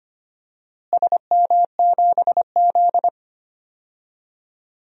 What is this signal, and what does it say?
Morse code, heard as a keyed tone, about 25 wpm, 700 Hz. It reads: SM7Z
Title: DXCC-100 Call Signs - 4 Characters - Repeated 3x - 25wpm